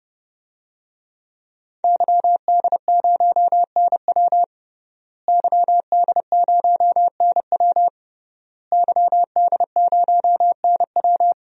Morse code, heard as a keyed tone, about 32 wpm, 700 Hz. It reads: YD0NW YD0NW YD0NW
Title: DXCC-100 Call Signs - 5 Characters - Repeated 3x - 30wpm